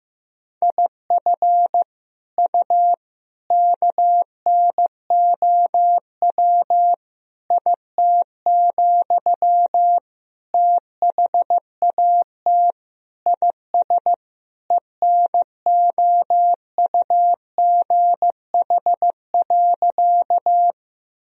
Morse code, heard as a keyed tone, about 16 wpm, 700 Hz. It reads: IF U KNOW IT, THAT IS ENOUGH.
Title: Sentences from Top 400 Words and Abbreviations 15wpm